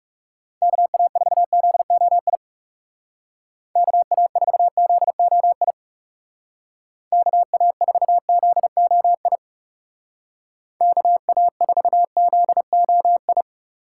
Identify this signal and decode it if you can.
Morse code, heard as a keyed tone, about 40 wpm, 700 Hz. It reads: KA4ZOI KA4ZOI KA4ZOI KA4ZOI